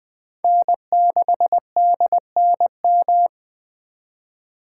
Morse code, heard as a keyed tone, about 20 wpm, 700 Hz. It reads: N6DNM
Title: 2023 SOTA Chaser Call Signs - 20wpm